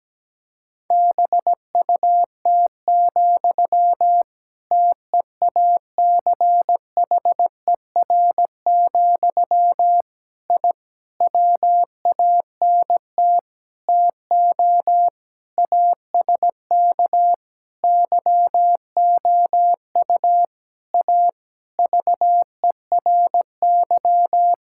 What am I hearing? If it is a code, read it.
Morse code, heard as a keyed tone, about 17 wpm, 700 Hz. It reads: BUT, TEACHER, I WANT TO ASK YOU A VERY
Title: Sentences from Top 1000 Words - Review Part 6 - 17wpm